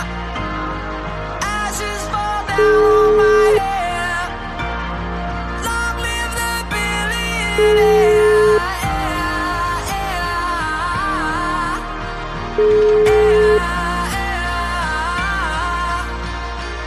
Ringback tone (Europe, 425 Hz), with noise.